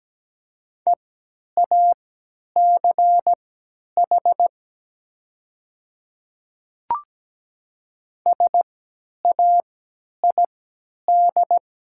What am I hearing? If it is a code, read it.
Morse code, heard as a keyed tone, about 17 wpm, spaced as for 10 wpm, 700 Hz. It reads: EACH SAID
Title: 3x Character Spacing - Top 200 Words - 17wpm